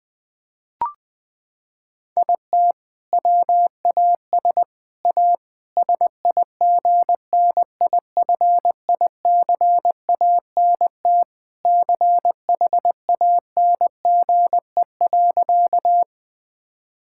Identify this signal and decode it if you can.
Morse code, heard as a keyed tone, about 21 wpm, 700 Hz. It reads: IT WAS A SIGNIFICANT CHANGE.
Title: Sentences from Top 1000 Words - Review Part 6 - 20wpm